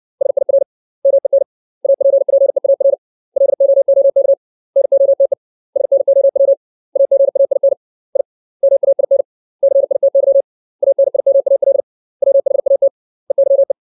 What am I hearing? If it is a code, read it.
Morse code, heard as a keyed tone, about 60 wpm, 550 Hz. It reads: SIR MEN APPEAR LOOK NOTE SNOW AGAIN I MAIN CITY ANIMAL WHAT EYE
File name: 632 words / 632 Words 60WPM